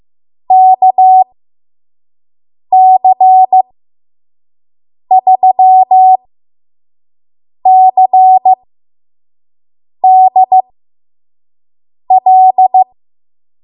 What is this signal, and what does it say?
Morse code, heard as a keyed tone, about 15 wpm, spaced as for 5 wpm, 750 Hz. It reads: KC3CDL
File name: Morse / 05WPM